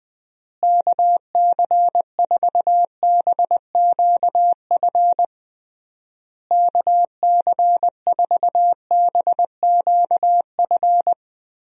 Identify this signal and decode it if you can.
Morse code, heard as a keyed tone, about 21 wpm, 700 Hz. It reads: KC4BQF KC4BQF
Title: USA Call Signs - 6 Characters - Repeated 3x - 20wpm